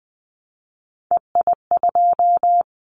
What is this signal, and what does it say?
Morse code, heard as a keyed tone, about 20 wpm, 700 Hz. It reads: EI2
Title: Callsign Prefixes for Top 50 DXCC - 20wpm